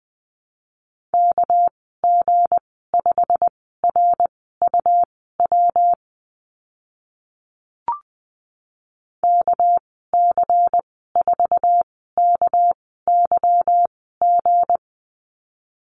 Morse code, heard as a keyed tone, about 20 wpm, spaced as for 14 wpm, 700 Hz. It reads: KG5RUW KC4KYG
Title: 219 Lesson 35g - Number 0 - Call Signs 20wpm